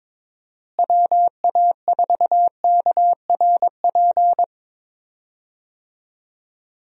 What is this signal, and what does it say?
Morse code, heard as a keyed tone, about 22 wpm, 700 Hz. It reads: WA4KRP
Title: USA Call Signs - 6 Characters - Repeated 3x - 22wpm